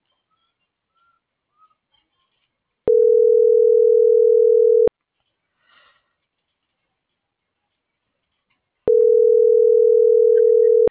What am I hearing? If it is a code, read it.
Ringback tone (North America), 440 + 480 Hz.